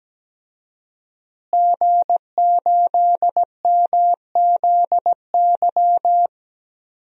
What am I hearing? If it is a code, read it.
Morse code, heard as a keyed tone, about 17 wpm, 700 Hz. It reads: G8MZY
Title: DXCC-100 Call Signs - 5 Characters - Repeated 2x - 17wpm